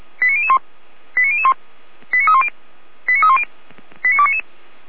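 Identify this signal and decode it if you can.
Five-tone selective call (ZVEI), heard as quick runs of five tones, tones of about 70 ms. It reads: 89001 89001 89310 89310 89299
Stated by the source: ZVEI five-tone selective call received with an SDR.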